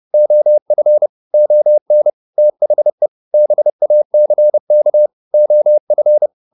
Morse code, heard as a keyed tone, about 30 wpm, 600 Hz. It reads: OF ON THE BACK OF
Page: advanced-practice-files